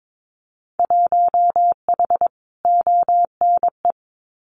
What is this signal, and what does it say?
Morse code, heard as a keyed tone, about 22 wpm, 700 Hz. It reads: 1H ONE
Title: Winter Field Day Exchanges Encore 22wpm